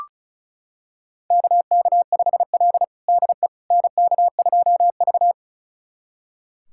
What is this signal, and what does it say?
Morse code, heard as a keyed tone, about 36 wpm, 700 Hz. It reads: KK5L DE NK2V
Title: USA Callsign DE Callsign - 4 Characters - 35wpm